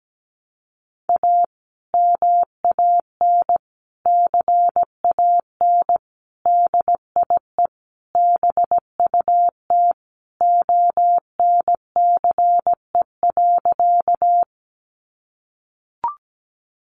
Morse code, heard as a keyed tone, about 17 wpm, 700 Hz. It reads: A MAN CAN DIE BUT ONCE.
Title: Sentences from Top 1300 Words 17wpm